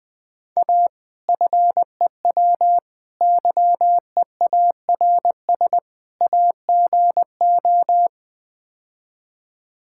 Morse code, heard as a keyed tone, about 20 wpm, 700 Hz. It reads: A FEW YEARS AGO
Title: Build A Phrase - 4-Word Phrases - 20wpm